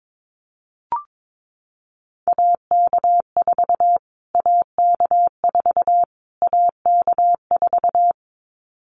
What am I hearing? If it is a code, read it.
Morse code, heard as a keyed tone, about 22 wpm, 700 Hz. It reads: AK4 AK4 AK4
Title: Callsign Prefixes for Top 25 DXCC - Repeated 3x - 22wpm